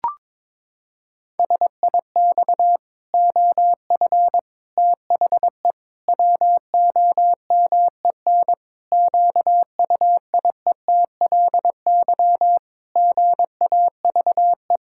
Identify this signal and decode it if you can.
Morse code, heard as a keyed tone, about 22 wpm, 700 Hz. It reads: SIX OF THE WOMEN QUIETLY GAVE
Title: Pangram Challenge v2 - 22wpm